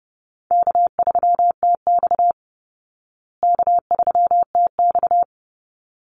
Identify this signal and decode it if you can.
Morse code, heard as a keyed tone, about 30 wpm, 700 Hz. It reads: K3TX K3TX
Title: Easy Call Signs Repeated 2x - 30wpm